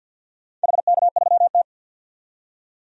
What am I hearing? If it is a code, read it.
Morse code, heard as a keyed tone, about 50 wpm, 700 Hz. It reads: HK3T
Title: DXCC-100 Call Signs - 4 Characters - 50wpm